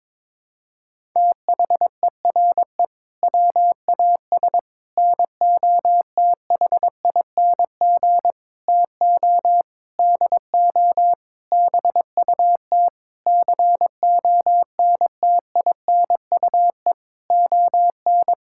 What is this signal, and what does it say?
Morse code, heard as a keyed tone, about 22 wpm, 700 Hz. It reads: THERE WAS NOTHING TO DO BUT CONTINUE ON
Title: Sentences from Top 1000 Words - Review Part 6 - 22wpm